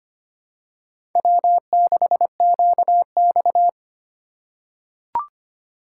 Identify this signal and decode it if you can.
Morse code, heard as a keyed tone, about 25 wpm, 700 Hz. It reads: W6QX
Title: USA Call Signs - 4 Characters - 4x Character Spacing - 25wpm